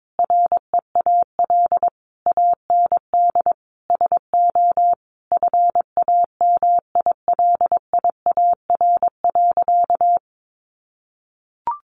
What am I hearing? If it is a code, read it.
Morse code, heard as a keyed tone, about 22 wpm, 700 Hz. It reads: REAL AND SO FAMILIAR.
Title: Sentences from Top 1000 Words 22wpm